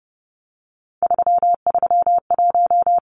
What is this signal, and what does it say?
Morse code, heard as a keyed tone, about 30 wpm, 700 Hz. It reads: 331
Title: Numbers - 3-Digit 30wpm